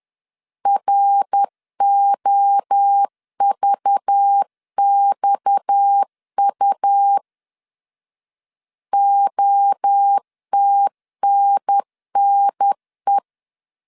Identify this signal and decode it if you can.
Morse code, heard as a keyed tone, about 10 wpm, 800 Hz. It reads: ROVXU OTNNE
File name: CW / DET / 1 6 1500C S B 10wpm 4137